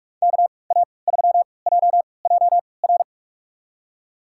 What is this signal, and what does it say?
Morse code, heard as a keyed tone, about 47 wpm, spaced as for 26 wpm, 690 Hz. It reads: KA3JJR
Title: USA Call Signs - 6 Characters - 3x Character Spacing - 45wpm